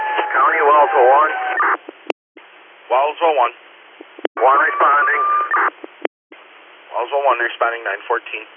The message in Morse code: none

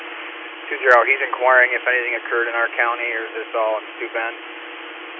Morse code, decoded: none

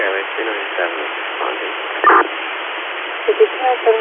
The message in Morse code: none

